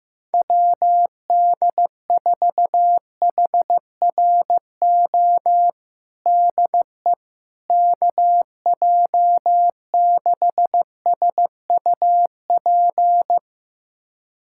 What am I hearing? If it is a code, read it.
Morse code, heard as a keyed tone, about 15 wpm, 700 Hz. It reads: WD4HRO DE KJ6SUP
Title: USA Callsign DE Callsign 15wpm